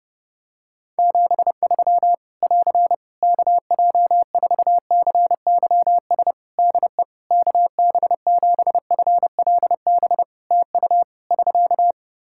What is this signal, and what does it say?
Morse code, heard as a keyed tone, about 30 wpm, 700 Hz. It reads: 73 <AR> KJ4CYS DE KB7FLB TU <SK>
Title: Practice QSO - Exchange 4 - Ending - 30wpm